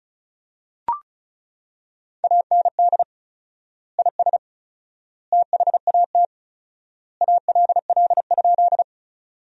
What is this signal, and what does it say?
Morse code, heard as a keyed tone, about 34 wpm, 700 Hz. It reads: AND IS THAT ALL?
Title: 4x Word-Spacing - Sentences from Top 100 Words - 35wpm